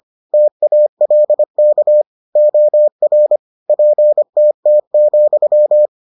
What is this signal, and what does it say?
Morse code, heard as a keyed tone, about 25 wpm, 600 Hz. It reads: TALK OR PTT,